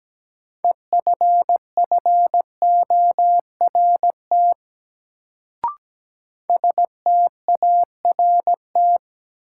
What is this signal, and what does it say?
Morse code, heard as a keyed tone, about 17 wpm, 700 Hz. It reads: EFFORT START
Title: Sets of 4 Words - New Word Drill - 17wpm